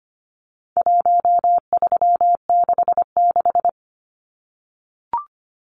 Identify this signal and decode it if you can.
Morse code, heard as a keyed tone, about 25 wpm, 700 Hz. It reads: 1366
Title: Numbers - Contest Style - 4-Digit 25wpm